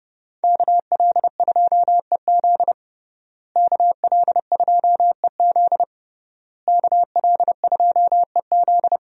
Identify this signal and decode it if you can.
Morse code, heard as a keyed tone, about 30 wpm, 700 Hz. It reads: KL2EZ KL2EZ KL2EZ
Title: USA Call Signs - 5 Characters - Repeated 3x - 30wpm